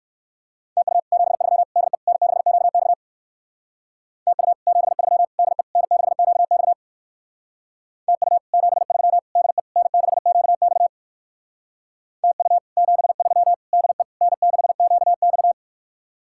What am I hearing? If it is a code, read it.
Morse code, heard as a keyed tone, about 58 wpm, 700 Hz. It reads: TU 73 DE N6QX TU 73 DE N6QX TU 73 DE N6QX TU 73 DE N6QX